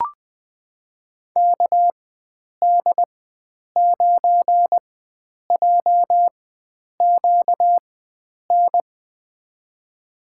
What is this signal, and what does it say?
Morse code, heard as a keyed tone, about 21 wpm, spaced as for 9 wpm, 700 Hz. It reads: KD9JQN
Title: USA Call Signs - 4x Character Spacing - 20wpm